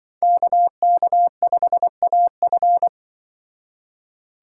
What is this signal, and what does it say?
Morse code, heard as a keyed tone, about 24 wpm, 700 Hz. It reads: KK5AF
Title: Speed Racing - USA 2x2 Call Signs 20wpm